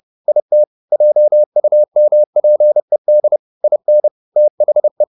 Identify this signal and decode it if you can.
Morse code, heard as a keyed tone, about 30 wpm, 600 Hz. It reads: IT JUMPED IN THE